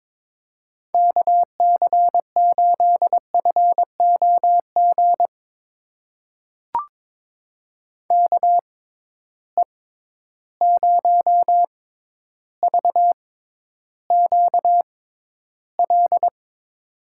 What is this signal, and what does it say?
Morse code, heard as a keyed tone, about 22 wpm, spaced as for 11 wpm, 700 Hz. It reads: KC8FOG KE0VQL
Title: USA Call Signs - 6 Characters - 6x Character Spacing - 22wpm